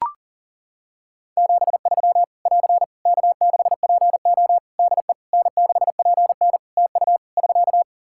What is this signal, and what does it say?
Morse code, heard as a keyed tone, about 37 wpm, 700 Hz. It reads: 73 <AR> K6PO DE N6PN TU <SK>
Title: K1USN SST QSO Practice 40wpm